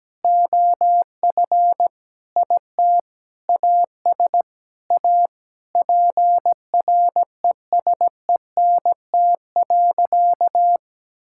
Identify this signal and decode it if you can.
Morse code, heard as a keyed tone, about 17 wpm, 700 Hz. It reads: OF IT AS A PRESENT.
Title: Sentences from Top 1000 Words - Review Part 5 - 17wpm